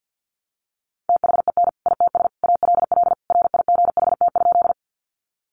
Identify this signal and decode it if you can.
Morse code, heard as a keyed tone, about 50 wpm, 700 Hz. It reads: THEN ITS ALL RIGHT?
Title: Sentences from Top 600 Words 50wpm